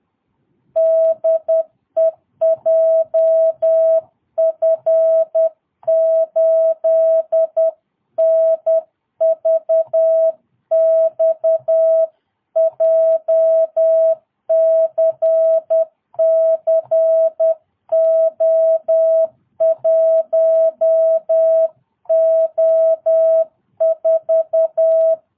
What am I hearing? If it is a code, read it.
Morse code, heard as a keyed tone, about 10 wpm, 650 Hz. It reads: DE JF8NVX JCC O1O4
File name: jh8cbh / 2022cw